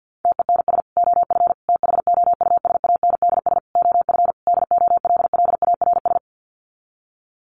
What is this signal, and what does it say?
Morse code, heard as a keyed tone, about 50 wpm, 700 Hz. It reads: TENS OF THOUSANDS OF DOLLARS
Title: Sets of 5 Words Encore2 50wpm